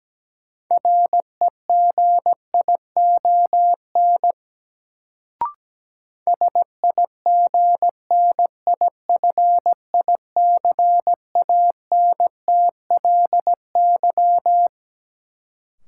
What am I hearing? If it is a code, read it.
Morse code, heard as a keyed tone, about 17 wpm, 700 Hz. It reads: REGION SIGNIFICANTLY
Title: Sentences from Top 1500 Words - New Word Drill Alt - 17wpm